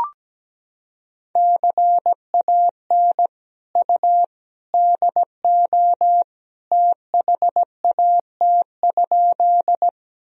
Morse code, heard as a keyed tone, about 17 wpm, 700 Hz. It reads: CAN U DO THAT?
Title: Sentences from Top 100 Words and Abbreviations 17wpm